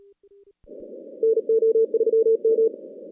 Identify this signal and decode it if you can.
Morse code, heard as a keyed tone, about 38 wpm, 450 Hz. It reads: NO3M
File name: contests / CW-SS-k5tr- -2100